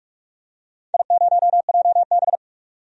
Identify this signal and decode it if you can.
Morse code, heard as a keyed tone, about 41 wpm, 700 Hz. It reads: I0JB
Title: DXCC-100 Call Signs - 4 Characters - 45wpm